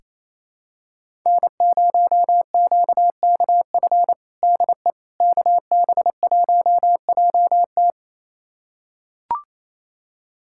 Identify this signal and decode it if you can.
Morse code, heard as a keyed tone, about 28 wpm, 700 Hz. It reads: N0QKF DE KB1JT
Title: USA Callsign DE Callsign - 5 Characters - 28wpm